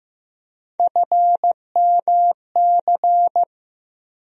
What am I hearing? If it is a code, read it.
Morse code, heard as a keyed tone, about 15 wpm, 700 Hz. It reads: FMC
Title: USA Call Sign Postfixes - 3 Letters 15wpm